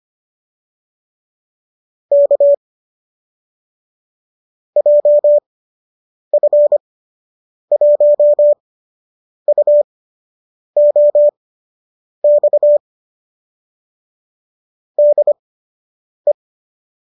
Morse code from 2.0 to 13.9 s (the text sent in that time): K JF1UOX